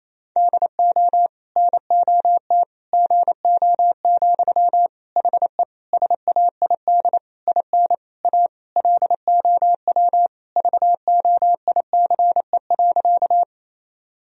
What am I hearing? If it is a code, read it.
Morse code, heard as a keyed tone, about 28 wpm, 700 Hz. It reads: DO NOT GO, HE SAID IN A LOW VOICE.
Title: Sentences from Top 1000 Words - Review Part 1 - 28wpm